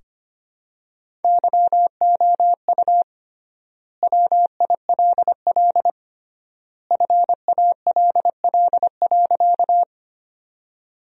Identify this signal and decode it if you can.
Morse code, heard as a keyed tone, about 25 wpm, 700 Hz. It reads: YOU WILL FALL.
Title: 3x Word-Spacing - Sentences from Top 500 Words - 25wpm